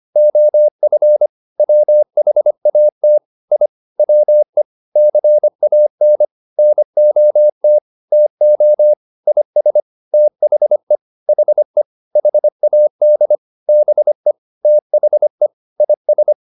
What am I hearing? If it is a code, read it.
Morse code, heard as a keyed tone, about 25 wpm, 600 Hz. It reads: OF WHAT I WE CAN NOT TO IS THE HE HAD BE THE IS